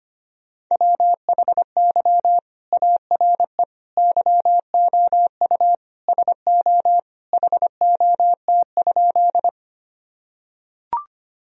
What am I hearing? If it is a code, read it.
Morse code, heard as a keyed tone, about 25 wpm, 700 Hz. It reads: WHY ARE YOU SO HOT?
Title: Sentences from Top 1000 Words - Review Part 3 - 25wpm